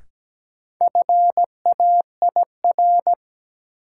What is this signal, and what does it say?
Morse code, heard as a keyed tone, about 17 wpm, 700 Hz. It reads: FAIR
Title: Sentences from Top 1300 Words - New Word Drill Alt - 17wpm